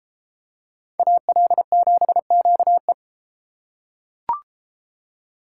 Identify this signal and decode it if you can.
Morse code, heard as a keyed tone, about 32 wpm, 700 Hz. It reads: AL7QE